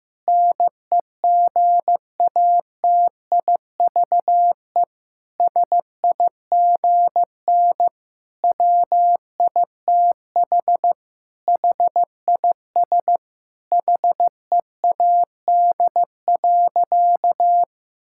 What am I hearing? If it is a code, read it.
Morse code, heard as a keyed tone, about 15 wpm, 700 Hz. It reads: NEGATIVE SIGN WITH HIS HEAD.
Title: Sentences from Top 1000 Words - Review Part 6 - 15wpm